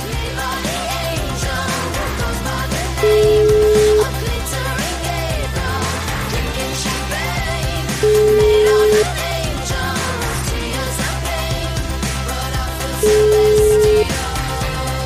Ringback tone (Europe, 425 Hz), with noise.